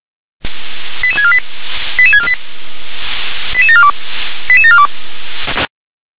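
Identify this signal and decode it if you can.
Five-tone selective call (ZVEI), heard as quick runs of five tones, tones of about 80 ms. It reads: 88648 88648 88742 88742
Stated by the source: ZVEI five-tone call received on SDR.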